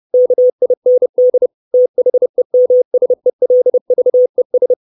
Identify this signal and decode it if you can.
Morse code, heard as a keyed tone, about 30 wpm, 500 Hz. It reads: KIND THEMSELVES
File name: 632 Words 30WPM